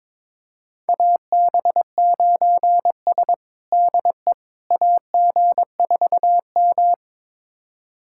Morse code, heard as a keyed tone, about 21 wpm, 700 Hz. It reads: AB9S DE AG4M
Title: USA Callsign DE Callsign - 4 Characters - 22wpm